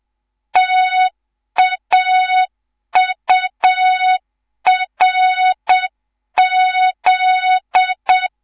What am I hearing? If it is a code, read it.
Morse code, heard as a keyed tone, about 7 wpm, 3700 Hz. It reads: TAURZ
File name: Morze-HA2MN / hang / 11 GYAKORLAT